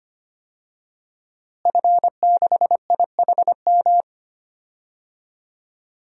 Morse code, heard as a keyed tone, about 25 wpm, 700 Hz. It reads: F6IHM
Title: DXCC-100 Call Signs - 5 Characters - Repeated 3x - 25wpm